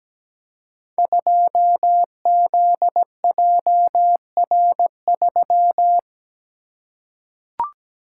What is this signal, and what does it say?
Morse code, heard as a keyed tone, about 17 wpm, 700 Hz. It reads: 2ZJR3
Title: Code Groups - 5 Letter:Numbers - 17wpm